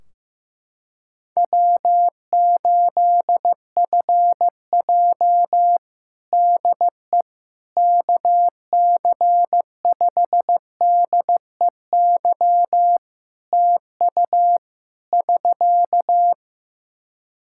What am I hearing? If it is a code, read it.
Morse code, heard as a keyed tone, about 16 wpm, 700 Hz. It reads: W8FJ DE KC5DEY TU <SK>
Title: Practice QSO - Exchange 4 - Ending - 15wpm